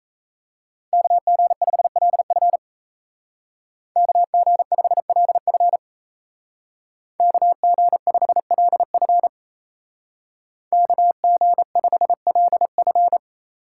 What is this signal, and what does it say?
Morse code, heard as a keyed tone, about 35 wpm, 700 Hz. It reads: KG5LF KG5LF KG5LF KG5LF